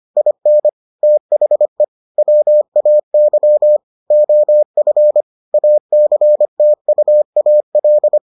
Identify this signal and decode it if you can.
Morse code, heard as a keyed tone, about 25 wpm, 600 Hz. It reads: IN THE WAY OF ACTUAL